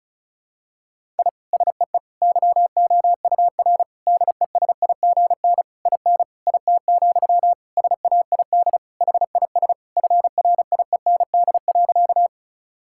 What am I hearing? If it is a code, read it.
Morse code, heard as a keyed tone, about 36 wpm, 700 Hz. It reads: I SEE YOUR DESIGN IN IT, SAID HIS FRIEND.
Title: Sentences from Top 1000 Words - Review Part 5 - 35wpm